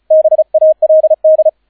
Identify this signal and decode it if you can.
Morse code, heard as a keyed tone, about 36 wpm, 630 Hz, with radio noise. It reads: BALD